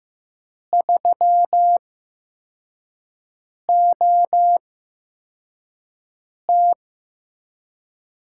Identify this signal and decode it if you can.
Morse code, heard as a keyed tone, about 15 wpm, spaced as for 4 wpm, 700 Hz. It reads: 3OT
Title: USA Call Signs - 8x Character Spacing - 15wpm